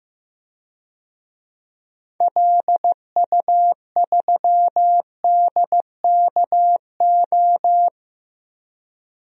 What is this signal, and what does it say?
Morse code, heard as a keyed tone, about 15 wpm, 700 Hz. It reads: LU3DKO
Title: DXCC-100 Call Signs - 6 Characters - 15wpm